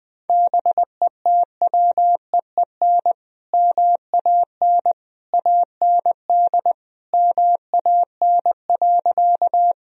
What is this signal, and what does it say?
Morse code, heard as a keyed tone, about 20 wpm, 700 Hz. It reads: BETWEEN MAN AND MAN.
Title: Sentences from Top 1000 Words - Review Part 1 - 20wpm